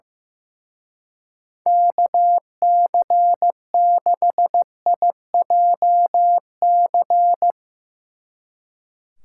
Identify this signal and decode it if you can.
Morse code, heard as a keyed tone, about 16 wpm, 700 Hz. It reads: KC6IJC
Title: Speed Racing - USA Call Signs 15wpm